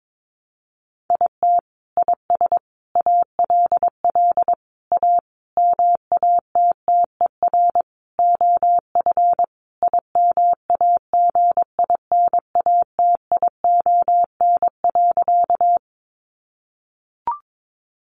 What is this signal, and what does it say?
Morse code, heard as a keyed tone, about 22 wpm, 700 Hz. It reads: IT IS ALL A MATTER OF IMAGINATION.
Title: Sentences from Top 1300 Words 22wpm